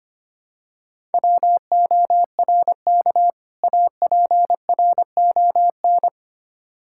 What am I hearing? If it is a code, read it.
Morse code, heard as a keyed tone, about 25 wpm, 700 Hz. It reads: WORK APRON
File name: practice / 25wpm / Difficult Words - Speed-Racing with Hints - 25wpm